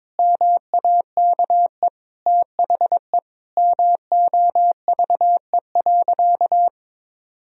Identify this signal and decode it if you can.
Morse code, heard as a keyed tone, about 22 wpm, 700 Hz. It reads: MAKE THE MOVE.
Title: Sentences from Top 1000 Words - Review Part 1 - 22wpm